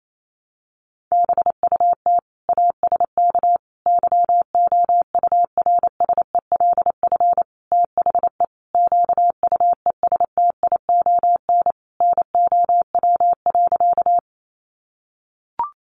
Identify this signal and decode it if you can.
Morse code, heard as a keyed tone, about 28 wpm, 700 Hz. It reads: BUT ASK YOURSELF THE QUESTION NOW.
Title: Sentences from Top 600 Words 28wpm